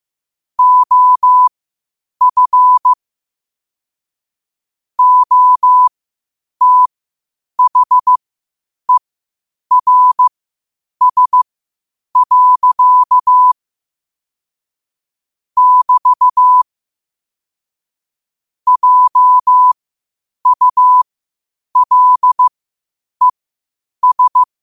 Morse code, heard as a keyed tone, about 15 wpm, spaced as for 8 wpm, 1000 Hz. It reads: OF OTHERS. <BT> JULES
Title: 8 WPM morse code quotes for Thu, 29 Feb 2024